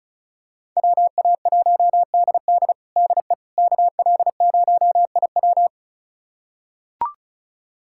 Morse code, heard as a keyed tone, about 34 wpm, 700 Hz. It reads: WA1DD DE KL0IW
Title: USA Callsign DE Callsign - 5 Characters - 35wpm